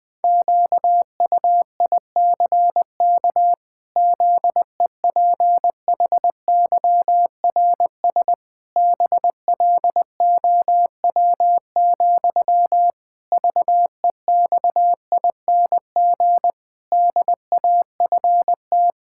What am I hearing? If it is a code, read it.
Morse code, heard as a keyed tone, about 19 wpm, 700 Hz. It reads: QUICK ZEPHYRS BLOW, VEXING DAFT
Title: Pangram Challenge v2 - 20wpm